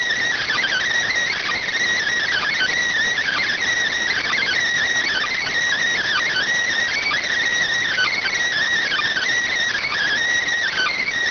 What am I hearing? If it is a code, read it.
Emergency Alert System (SAME) tones (header bursts), heard as harsh buzzing bursts. It is heard from 0.4 to 10.1 s.